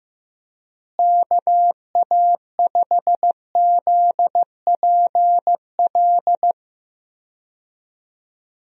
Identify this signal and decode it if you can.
Morse code, heard as a keyed tone, about 15 wpm, 700 Hz. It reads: KA5ZPL
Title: USA Call Signs - 6 Characters - Repeated 2x - 15wpm